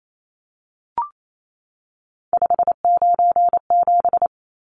Morse code, heard as a keyed tone, about 28 wpm, 700 Hz. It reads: 597
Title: Numbers - 3-Digit 28wpm